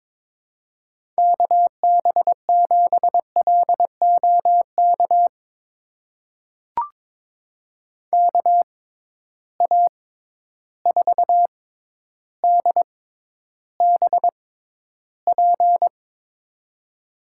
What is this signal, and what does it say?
Morse code, heard as a keyed tone, about 22 wpm, spaced as for 11 wpm, 700 Hz. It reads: KB7LOK KA4DBP
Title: USA Call Signs - 6 Characters - 6x Character Spacing - 22wpm